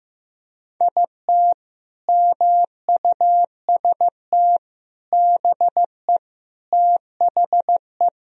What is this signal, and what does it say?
Morse code, heard as a keyed tone, about 15 wpm, 700 Hz. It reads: IT MUST BE THE
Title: Sentences from Top 1000 Words - Review Part 4 - 15wpm